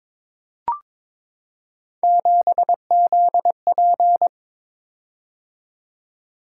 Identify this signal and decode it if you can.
Morse code, heard as a keyed tone, about 22 wpm, 700 Hz. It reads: 7ZP
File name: practice / 22wpm / Three Letter-Number Combinations - 22wpm